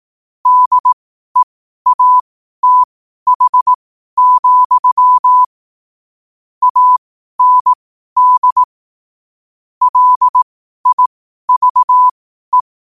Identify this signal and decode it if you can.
Morse code, heard as a keyed tone, about 18 wpm, spaced as for 12 wpm, 1000 Hz. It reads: DEATH, AND LIVE
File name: qotdfiles / 12wpm